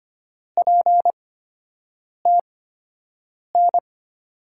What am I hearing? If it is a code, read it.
Morse code, heard as a keyed tone, about 25 wpm, spaced as for 7 wpm, 700 Hz. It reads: PTN
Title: USA Call Signs - 5 Characters - 8x Character Spacing - 25wpm